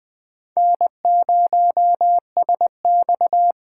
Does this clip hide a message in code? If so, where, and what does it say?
Yes: Morse code from 0.6 to 3.5 s, heard as a keyed tone, about 20 wpm, 700 Hz. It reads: N0SX